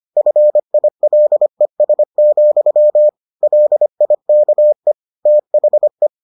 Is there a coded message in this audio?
Yes: Morse code from 0.2 to 6.1 s, heard as a keyed tone, about 25 wpm, 600 Hz. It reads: FILES, LIKE THE